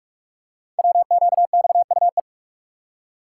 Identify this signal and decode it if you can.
Morse code, heard as a keyed tone, about 44 wpm, 700 Hz. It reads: WQXUE
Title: Code Groups - 5 Letters - 45wpm